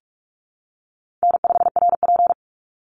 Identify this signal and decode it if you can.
Morse code, heard as a keyed tone, about 44 wpm, 700 Hz. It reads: N5RP
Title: Easy Call Signs Repeated 3x - 45wpm